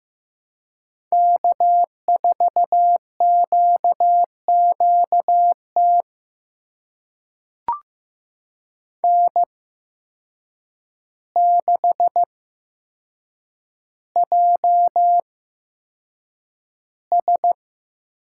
Morse code, heard as a keyed tone, about 15 wpm, spaced as for 7 wpm, 700 Hz. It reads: K4QQT N6JS
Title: USA Call Signs - 5 Characters - 8x Character Spacing - 15wpm